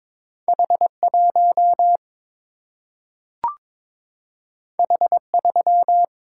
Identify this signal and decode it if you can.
Morse code, heard as a keyed tone, about 22 wpm, 700 Hz. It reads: H 1 H 3
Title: Single-Letter Single-Number 22wpm